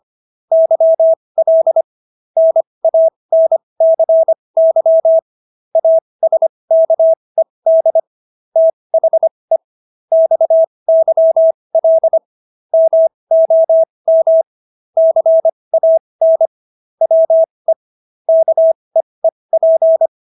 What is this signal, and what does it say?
Morse code, heard as a keyed tone, about 25 wpm, spaced as for 20 wpm, 650 Hz. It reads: YL NANCY ASKED THE XYL MOM CAN WE KEEP